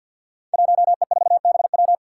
Morse code, heard as a keyed tone, about 51 wpm, 700 Hz. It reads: 1E4BW
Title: Five Challenging Letter-Number Combinations 50wpm